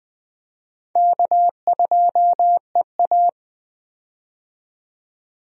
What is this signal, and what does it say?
Morse code, heard as a keyed tone, about 20 wpm, 700 Hz. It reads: K2EA
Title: USA Call Signs - 4 Characters - 20wpm